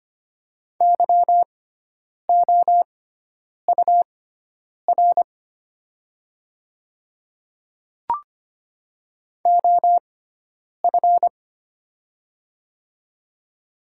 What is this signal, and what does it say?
Morse code, heard as a keyed tone, about 25 wpm, spaced as for 9 wpm, 700 Hz. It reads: YOUR OF
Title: 6x Character Spacing - Top 100 Words - 25wpm